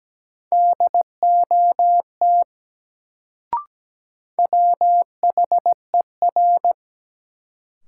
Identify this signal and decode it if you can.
Morse code, heard as a keyed tone, about 17 wpm, 700 Hz. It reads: DOT WHER
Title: Sentences from Top 1500 Words - New Word Drill Alt - 17wpm